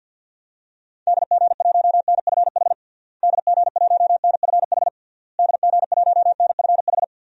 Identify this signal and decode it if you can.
Morse code, heard as a keyed tone, about 52 wpm, 700 Hz. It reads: DG1NFH DG1NFH DG1NFH
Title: DXCC-100 Call Signs - 6 Characters - Repeated 3x - 50wpm